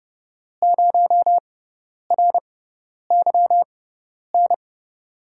Morse code, heard as a keyed tone, about 31 wpm, spaced as for 10 wpm, 700 Hz. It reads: 0RYN
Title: USA Call Signs - 5 Characters - 6x Character Spacing - 30wpm